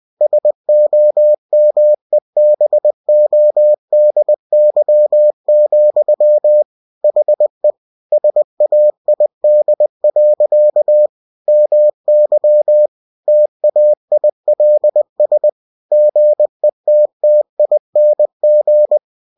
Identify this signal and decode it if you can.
Morse code, heard as a keyed tone, about 20 wpm, 600 Hz. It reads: SOMEBODY, HE SAID. MY TAILS GETTING
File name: cw-books / winnie-the-pooh / winnie-20wpm-page-0087